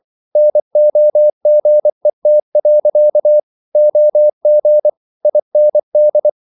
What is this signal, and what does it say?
Morse code, heard as a keyed tone, about 24 wpm, 600 Hz. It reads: NOGET. OG IND
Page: Ferie 24 wpm